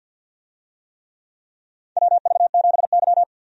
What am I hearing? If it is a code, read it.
Morse code, heard as a keyed tone, about 52 wpm, 700 Hz. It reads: WV7Y
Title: USA Call Signs - 4 Characters - 50wpm